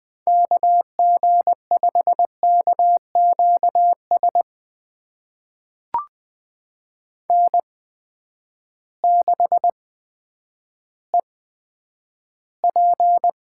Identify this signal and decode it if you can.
Morse code, heard as a keyed tone, about 21 wpm, spaced as for 10 wpm, 700 Hz. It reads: KG5KQS N6EP
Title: USA Call Signs - 8x Character Spacing - 20wpm